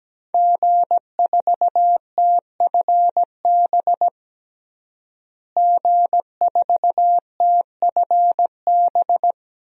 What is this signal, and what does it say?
Morse code, heard as a keyed tone, about 17 wpm, 700 Hz. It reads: G4TFB G4TFB
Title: DXCC-100 Call Signs - 5 Characters - Repeated 3x - 17wpm